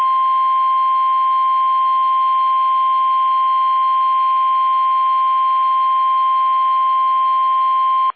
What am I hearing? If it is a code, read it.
Emergency Alert System (SAME) tones (the 1050 Hz attention tone). It is heard from 0.3 to 7.4 s.
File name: eas / KUSO / KUSO